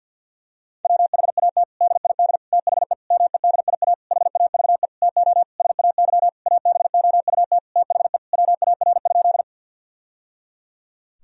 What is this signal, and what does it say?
Morse code, heard as a keyed tone, about 50 wpm, 700 Hz. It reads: WHAT DID THE MEDIA HAVE TO SAY ABOUT THE WAR?
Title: Sentences from Top 1000 Words - Review Part 4 - 50wpm